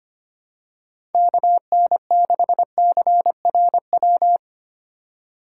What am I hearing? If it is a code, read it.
Morse code, heard as a keyed tone, about 25 wpm, 700 Hz. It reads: KN6CRW
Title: USA Call Signs - 2x Character Spacing - 25wpm